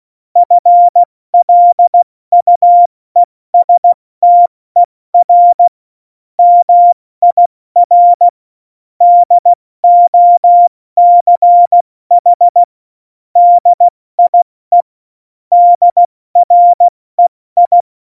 Morse code, heard as a keyed tone, about 16 wpm, 700 Hz. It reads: FLUESTER MIR DOCH DIE DREI
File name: Morsekurs / Witze / Witz 27. August 2019, 70 F80